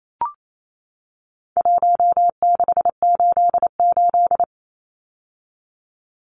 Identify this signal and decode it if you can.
Morse code, heard as a keyed tone, about 28 wpm, 700 Hz. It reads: 1688
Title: Numbers - Contest Style - 4-Digit 28wpm